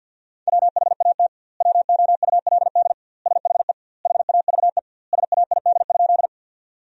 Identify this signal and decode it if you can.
Morse code, heard as a keyed tone, about 50 wpm, 700 Hz. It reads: WHAT WOULD SHE HAVE SAID?
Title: 2x Word-Spacing - Sentences from Top 100 Words - 50wpm